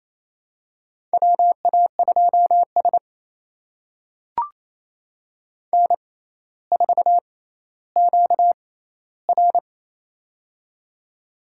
Morse code, heard as a keyed tone, about 27 wpm, spaced as for 14 wpm, 700 Hz. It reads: WA2S N4QR
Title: USA Call Signs - 4 Characters - 6x Character Spacing - 28wpm